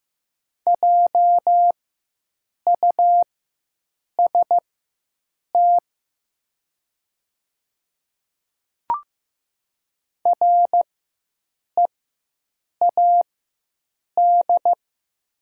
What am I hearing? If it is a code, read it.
Morse code, heard as a keyed tone, about 15 wpm, spaced as for 7 wpm, 700 Hz. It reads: JUST READ
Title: 4x Character Spacing - Top 200 Words - 15wpm